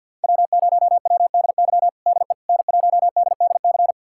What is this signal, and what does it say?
Morse code, heard as a keyed tone, about 52 wpm, 700 Hz. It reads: W0WDY DE N1DDC
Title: USA Callsign DE Callsign - 5 Characters - 50wpm